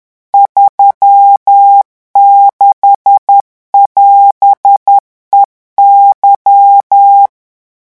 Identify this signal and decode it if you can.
Morse code, heard as a keyed tone, about 10 wpm, 800 Hz. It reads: 36<AS>EY
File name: morse mp3 / revision16-vitesse 10 mots